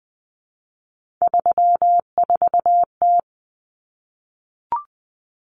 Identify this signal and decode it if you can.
Morse code, heard as a keyed tone, about 20 wpm, 700 Hz. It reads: 34T
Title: Common Cut Numbers - 3-Digit 20wpm